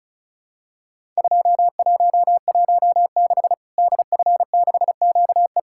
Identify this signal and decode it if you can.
Morse code, heard as a keyed tone, about 36 wpm, 700 Hz. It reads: 2116 DF6QE
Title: Worked All Europe DX Contest - QTC Exchange - 35wpm